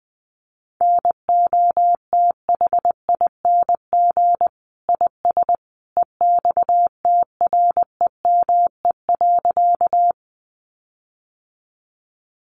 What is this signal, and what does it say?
Morse code, heard as a keyed tone, about 20 wpm, 700 Hz. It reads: NOTHING IS EXTREME.
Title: Sentences from Top 1300 Words 20wpm